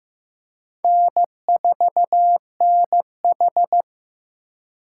Morse code, heard as a keyed tone, about 15 wpm, 700 Hz. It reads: N4NH
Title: USA Call Signs - 4 Characters - 3x Character Spacing - 15wpm